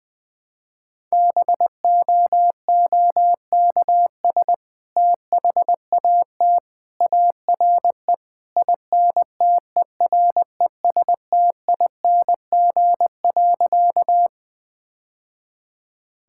Morse code, heard as a keyed tone, about 20 wpm, 700 Hz. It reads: BOOKS THAT ARE INTERESTING.
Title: Sentences with 2-Word Phrases - 20wpm